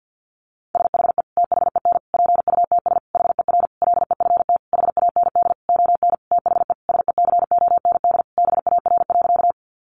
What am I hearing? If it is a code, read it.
Morse code, heard as a keyed tone, about 50 wpm, 700 Hz. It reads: SHE THEN PUTS HER LEFT HAND ON THE SECOND BAR.
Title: Easy Phrases 50wpm